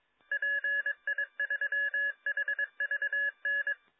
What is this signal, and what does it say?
Morse code, heard as a keyed tone, about 22 wpm, 1700 Hz. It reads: PI3HVN